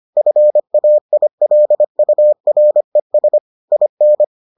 Morse code, heard as a keyed tone, about 25 wpm, 600 Hz. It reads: FAILURES IN